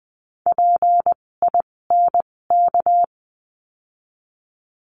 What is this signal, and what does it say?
Morse code, heard as a keyed tone, about 20 wpm, spaced as for 16 wpm, 700 Hz. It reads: PINK
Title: 256 Lesson 42a - Words 12-20wpm